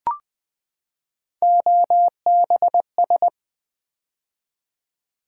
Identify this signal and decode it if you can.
Morse code, heard as a keyed tone, about 20 wpm, 700 Hz. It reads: OBS
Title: Build A Word - 3 Letters At A Time - 8-Letter Words - 20wpm